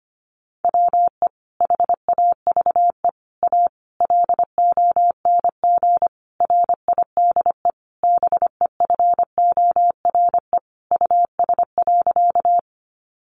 Morse code, heard as a keyed tone, about 25 wpm, 700 Hz. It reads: WE HAVE A LONG RIDE BEFORE US.
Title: Sentences from Top 1200 Words 25wpm